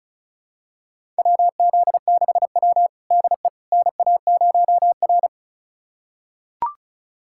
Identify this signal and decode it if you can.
Morse code, heard as a keyed tone, about 36 wpm, 700 Hz. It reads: WZ6W DE NA0R
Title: USA Callsign DE Callsign - 4 Characters - 35wpm